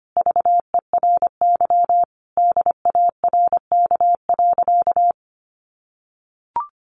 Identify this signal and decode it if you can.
Morse code, heard as a keyed tone, about 25 wpm, 700 Hz. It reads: VERY DARK.
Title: Sentences from Top 500 Words 25wpm